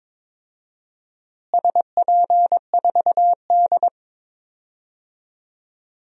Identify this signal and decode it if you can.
Morse code, heard as a keyed tone, about 22 wpm, 700 Hz. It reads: SP4D
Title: DXCC-100 Call Signs - 4 Characters - Repeated 3x - 22wpm